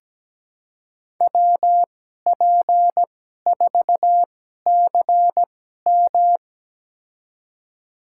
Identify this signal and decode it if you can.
Morse code, heard as a keyed tone, about 17 wpm, spaced as for 12 wpm, 700 Hz. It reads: WP4CM
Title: USA Call Signs - 5 Characters - 2x Character Spacing - 17wpm